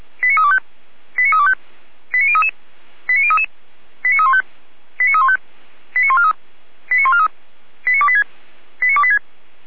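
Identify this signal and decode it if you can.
Five-tone selective call (ZVEI), heard as quick runs of five tones, tones of about 70 ms. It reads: 89325 89325 89033 89033 89315 89315 89143 89143 89286 89286
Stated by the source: ZVEI five-tone selective call 89503, received by SDR.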